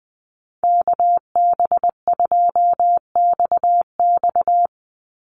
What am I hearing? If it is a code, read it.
Morse code, heard as a keyed tone, about 20 wpm, 700 Hz. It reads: KB2XX
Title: NA SOTA Chaser Callsigns 20wpm